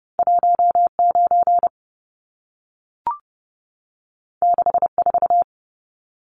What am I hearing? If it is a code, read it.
Morse code, heard as a keyed tone, about 30 wpm, 700 Hz. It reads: 19 64
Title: Numbers - 2-Digit 30wpm